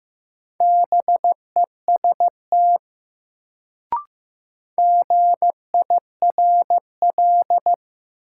Morse code, heard as a keyed tone, about 15 wpm, 700 Hz. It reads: BEST GIRL
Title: Sentences from Top 400 Words - New Word Drill - 15wpm